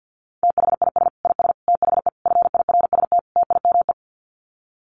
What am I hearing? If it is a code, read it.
Morse code, heard as a keyed tone, about 50 wpm, 700 Hz. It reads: THIS IS THE FIRST TIME
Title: Sets of 5 Words 50wpm